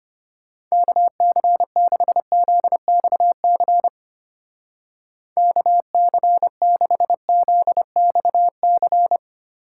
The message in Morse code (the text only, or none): KC6ZXC KC6ZXC